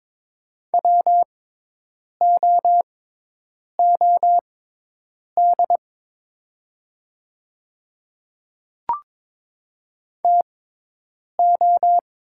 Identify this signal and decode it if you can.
Morse code, heard as a keyed tone, about 22 wpm, spaced as for 8 wpm, 700 Hz. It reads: WOOD TO
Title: 6x Character Spacing - Top 300 Words - 22wpm